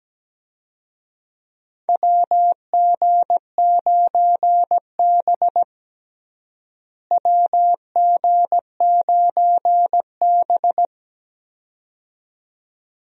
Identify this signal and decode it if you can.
Morse code, heard as a keyed tone, about 17 wpm, 700 Hz. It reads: WG9B WG9B
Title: USA Call Signs - 4 Characters - Repeated 2x - 17wpm